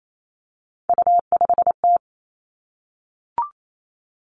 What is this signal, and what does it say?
Morse code, heard as a keyed tone, about 28 wpm, 700 Hz. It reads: U5T
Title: Three Challenging Letter-Number Combinations 28wpm